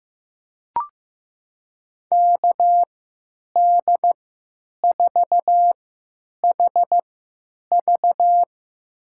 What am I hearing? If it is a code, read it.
Morse code, heard as a keyed tone, about 16 wpm, spaced as for 9 wpm, 700 Hz. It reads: KD4HV
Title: USA Call Signs - 5 Characters - 3x Character Spacing - 15wpm